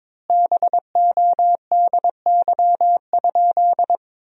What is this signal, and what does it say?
Morse code, heard as a keyed tone, about 22 wpm, 700 Hz. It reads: BODY?
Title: 4x Word-Spacing - Sentences from Top 400 Words - 22wpm